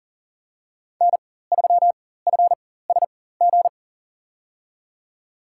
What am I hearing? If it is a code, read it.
Morse code, heard as a keyed tone, about 39 wpm, spaced as for 18 wpm, 700 Hz. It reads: N3FSG
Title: USA Call Signs - 4x Character Spacing - 40wpm